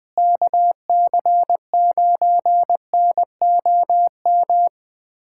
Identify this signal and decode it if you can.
Morse code, heard as a keyed tone, about 20 wpm, 700 Hz. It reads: KC9NOM
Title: USA Callsign DE Callsign - 6 Characters - 20wpm